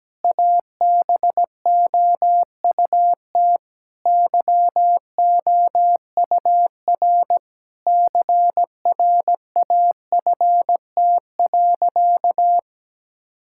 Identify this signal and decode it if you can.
Morse code, heard as a keyed tone, about 17 wpm, 700 Hz. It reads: ABOUT YOUR CRAFT.
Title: Sentences from Top 1000 Words - Review Part 6 - 17wpm